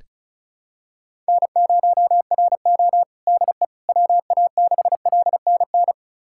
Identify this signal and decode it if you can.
Morse code, heard as a keyed tone, about 36 wpm, 700 Hz. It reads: N0RO DE WA6LNN
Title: USA Callsign DE Callsign 35wpm